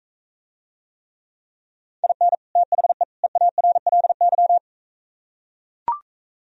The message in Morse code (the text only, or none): IN THE EARLY